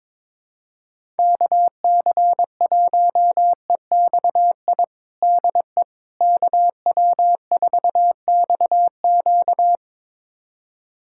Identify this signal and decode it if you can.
Morse code, heard as a keyed tone, about 22 wpm, 700 Hz. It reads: KC1EXI DE KW4XQ
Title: USA Callsign DE Callsign 22wpm